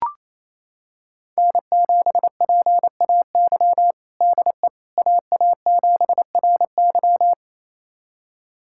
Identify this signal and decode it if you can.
Morse code, heard as a keyed tone, about 28 wpm, 700 Hz. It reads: N7PAY DE AA7RY
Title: USA Callsign DE Callsign - 5 Characters - 28wpm